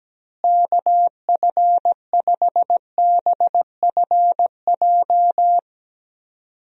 Morse code, heard as a keyed tone, about 17 wpm, 700 Hz. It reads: KF5BFJ
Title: USA Call Signs - 6 Characters - Repeated 3x - 17wpm